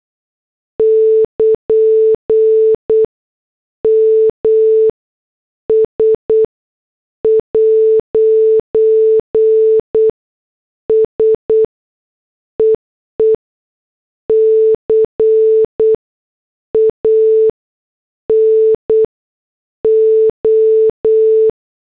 Morse code, heard as a keyed tone, about 8 wpm, spaced as for 6 wpm, 440 Hz. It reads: <KN> MS'S EE CANO
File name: morse / headlines